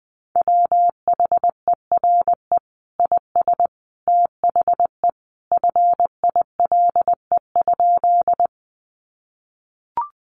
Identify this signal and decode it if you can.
Morse code, heard as a keyed tone, about 20 wpm, 700 Hz. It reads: WHERE IS THE FILE?
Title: Sentences from Top 1200 Words 20wpm